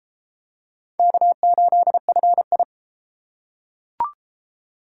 Morse code, heard as a keyed tone, about 31 wpm, 700 Hz. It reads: K8FI